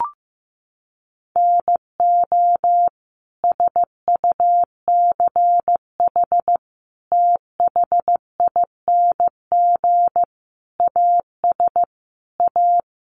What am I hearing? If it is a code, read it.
Morse code, heard as a keyed tone, about 15 wpm, 700 Hz. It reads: NO SUCH THING AS A
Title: Sets of 5 Words Encore 15wpm